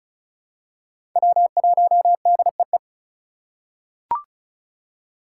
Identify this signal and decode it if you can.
Morse code, heard as a keyed tone, about 36 wpm, 700 Hz. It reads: W1DEE
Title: USA Call Signs - 5 Characters - 4x Character Spacing - 35wpm